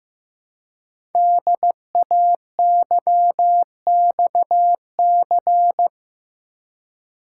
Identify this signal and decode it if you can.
Morse code, heard as a keyed tone, about 15 wpm, 700 Hz. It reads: DAYXC
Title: Code Groups - 5 Letters - 15wpm